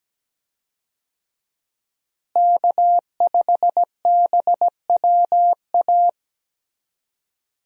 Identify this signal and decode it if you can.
Morse code, heard as a keyed tone, about 17 wpm, 700 Hz. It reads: K5BWA
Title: USA Call Signs - 5 Characters - Repeated 3x - 17wpm